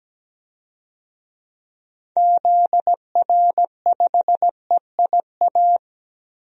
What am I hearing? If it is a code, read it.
Morse code, heard as a keyed tone, about 17 wpm, 700 Hz. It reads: ZR5EIA
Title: DXCC-100 Call Signs - 6 Characters - 17wpm